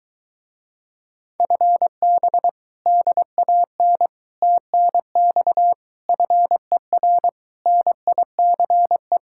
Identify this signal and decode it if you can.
Morse code, heard as a keyed tone, about 23 wpm, 700 Hz. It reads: FB DAN TNX FER NICE